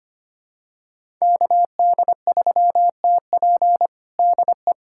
Morse code, heard as a keyed tone, about 25 wpm, 700 Hz. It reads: KD3TP DE
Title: USA Callsign DE Callsign - 5 Characters - 25wpm